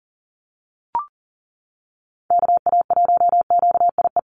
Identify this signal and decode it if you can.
Morse code, heard as a keyed tone, about 40 wpm, 700 Hz. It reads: KA1QIE
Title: USA Call Signs Repeated 3x 40wpm